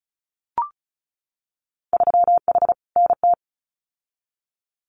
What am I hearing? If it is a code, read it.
Morse code, heard as a keyed tone, about 36 wpm, 700 Hz. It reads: 3H NT
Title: Winter Field Day Exchanges Encore 35wpm